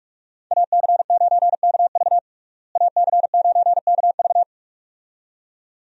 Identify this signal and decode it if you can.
Morse code, heard as a keyed tone, about 41 wpm, 700 Hz. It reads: AC9KV AC9KV
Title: USA Call Signs - 5 Characters - Repeated 2x - 45wpm